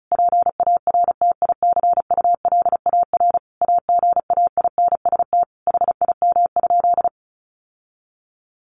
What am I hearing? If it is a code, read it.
Morse code, heard as a keyed tone, about 34 wpm, 700 Hz. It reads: PARTICULAR AGAINST HIM?
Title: Sentences from Top 1000 Words 35wpm